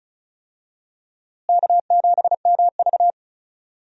Morse code, heard as a keyed tone, about 36 wpm, 700 Hz. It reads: K7MV
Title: USA Call Signs - 4 Characters - Repeated 2x - 35wpm